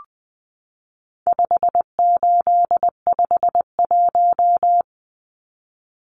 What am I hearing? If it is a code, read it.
Morse code, heard as a keyed tone, about 20 wpm, 700 Hz. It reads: 5851
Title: Numbers - 4-Digit 20wpm